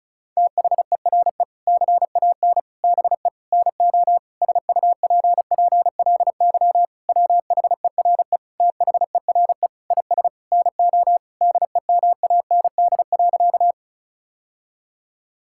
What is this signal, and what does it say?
Morse code, heard as a keyed tone, about 34 wpm, 700 Hz. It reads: THERE CAN BE NO SUPPLY WHERE THERE IS NO DEMAND.
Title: Sentences from Top 1000 Words - Review Part 5 - 35wpm